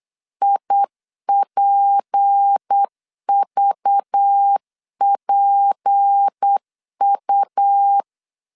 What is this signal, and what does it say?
Morse code, heard as a keyed tone, about 8 wpm, 800 Hz. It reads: IPVPU
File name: LEZIONI / 6 PKX0 / 6 P 8wpm 3025